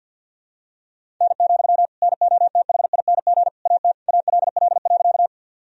Morse code, heard as a keyed tone, about 48 wpm, 700 Hz. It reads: N, NOTHING AT ALL.
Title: Sentences from Top 500 Words and Abbreviations 50wpm